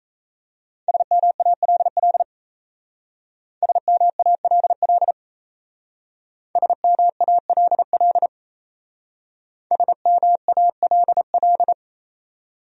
Morse code, heard as a keyed tone, about 36 wpm, 700 Hz. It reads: SMALL SMALL SMALL SMALL